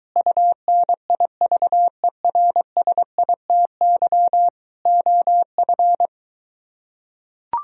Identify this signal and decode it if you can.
Morse code, heard as a keyed tone, about 23 wpm, 700 Hz. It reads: UNIVERSITY OF